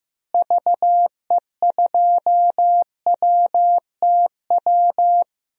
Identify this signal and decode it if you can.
Morse code, heard as a keyed tone, about 15 wpm, 700 Hz. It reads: VE2WTW
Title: DXCC-100 Call Signs - 6 Characters - Repeated 2x - 15wpm